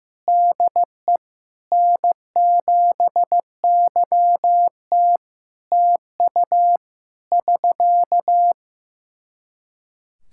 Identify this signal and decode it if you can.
Morse code, heard as a keyed tone, about 15 wpm, 700 Hz. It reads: DE N7YT TU <SK>
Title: Practice QSO - Exchange 4 - Ending - 15wpm